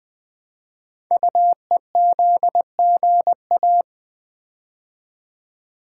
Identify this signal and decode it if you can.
Morse code, heard as a keyed tone, about 20 wpm, 700 Hz. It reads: UEZGA
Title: Code Groups - 5 Letter:Numbers - 20wpm